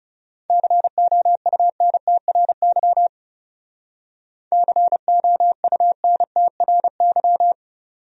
Morse code, heard as a keyed tone, about 34 wpm, 700 Hz. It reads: COUNTRY COUNTRY
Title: Speed-Racing Top 500 Words 30wpm